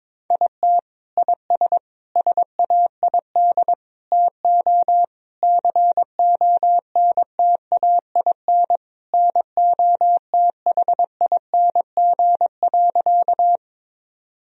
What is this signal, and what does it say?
Morse code, heard as a keyed tone, about 21 wpm, 700 Hz. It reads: IT IS SAID TO CONTAIN NOTHING.
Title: Sentences from Top 1000 Words - Review Part 2 - 22wpm